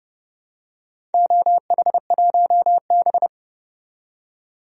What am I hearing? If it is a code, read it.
Morse code, heard as a keyed tone, about 30 wpm, 700 Hz. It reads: OH1B
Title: DXCC-100 Call Signs - 4 Characters - 30wpm